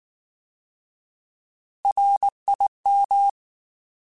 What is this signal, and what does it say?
Morse code, heard as a keyed tone, about 19 wpm, 800 Hz. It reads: RIM
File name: Alphabetical / Alphabetical mp3s / Rs @18